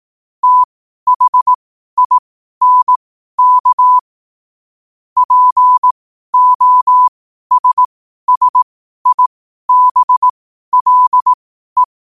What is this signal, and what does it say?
Morse code, heard as a keyed tone, about 18 wpm, spaced as for 12 wpm, 1000 Hz. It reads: THINK POSSIBLE